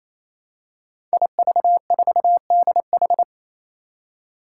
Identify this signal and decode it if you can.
Morse code, heard as a keyed tone, about 28 wpm, 700 Hz. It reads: IV4DH
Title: Five Challenging Letter-Number Combinations 28wpm